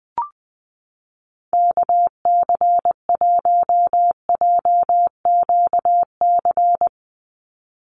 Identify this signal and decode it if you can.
Morse code, heard as a keyed tone, about 20 wpm, 700 Hz. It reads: KC1JQC
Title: USA Call Signs 20wpm